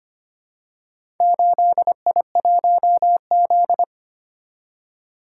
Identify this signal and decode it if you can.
Morse code, heard as a keyed tone, about 25 wpm, 700 Hz. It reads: 8I1Z
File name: practice / 25wpm / Four Letter-Number Combinations - 25wpm